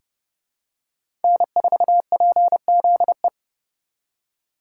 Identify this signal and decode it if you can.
Morse code, heard as a keyed tone, about 30 wpm, 700 Hz. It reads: N4PZE
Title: USA Call Signs - 5 Characters - 2x Character Spacing - 30wpm